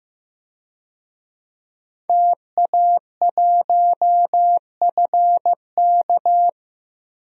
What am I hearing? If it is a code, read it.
Morse code, heard as a keyed tone, about 15 wpm, 700 Hz. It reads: TA1FK
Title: DXCC-100 Call Signs - 5 Characters - Repeated 3x - 15wpm